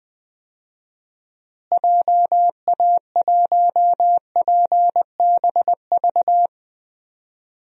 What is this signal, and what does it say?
Morse code, heard as a keyed tone, about 21 wpm, 700 Hz. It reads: JA1PBV
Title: DXCC-100 Call Signs - 6 Characters - Repeated 2x - 20wpm